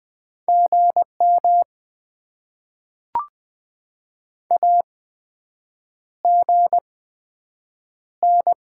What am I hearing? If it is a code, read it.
Morse code, heard as a keyed tone, about 20 wpm, spaced as for 7 wpm, 700 Hz. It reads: GM AGN
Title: Top 20 QSO Elements - 8x Character Spacing - 20wpm